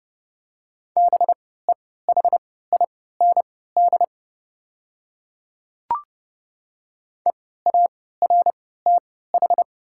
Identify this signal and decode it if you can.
Morse code, heard as a keyed tone, about 32 wpm, spaced as for 17 wpm, 700 Hz. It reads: BEHIND EARTH
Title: 3x Character Spacing - Top 500 Words - 30wpm